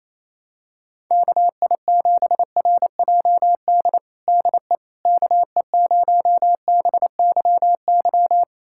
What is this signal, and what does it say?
Morse code, heard as a keyed tone, about 28 wpm, 700 Hz. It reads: KI7RJD DE KE0BYY
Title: USA Callsign DE Callsign - 6 Characters - 28wpm